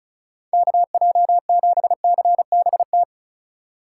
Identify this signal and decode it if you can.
Morse code, heard as a keyed tone, about 36 wpm, 700 Hz. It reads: KJ7CBT
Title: USA Call Signs - 6 Characters - 8x Character Spacing - 35wpm